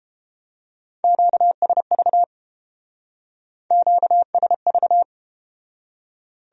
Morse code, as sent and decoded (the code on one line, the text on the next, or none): --.- ... ...- / --.- ... ...-
QSV QSV